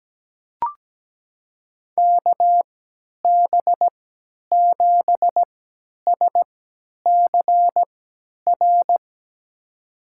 Morse code, heard as a keyed tone, about 17 wpm, spaced as for 10 wpm, 700 Hz. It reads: KB7SCR
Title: USA Call Signs - 3x Character Spacing - 17wpm